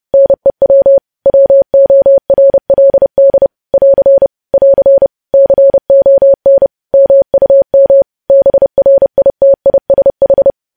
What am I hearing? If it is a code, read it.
Morse code, heard as a keyed tone, about 30 wpm, 550 Hz. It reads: NEW WORLD <AR> <AR> CON MUM BRITISH
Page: news headlines in morse code at 30 wpm